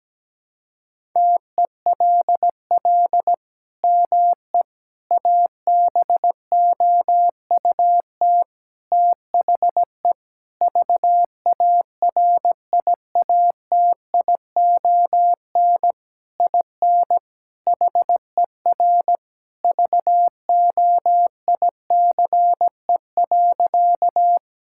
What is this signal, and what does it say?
Morse code, heard as a keyed tone, about 17 wpm, 700 Hz. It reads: TELL ME ABOUT THE VARIATION IN HER VOICE.
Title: Sentences from Top 1600 Words 17wpm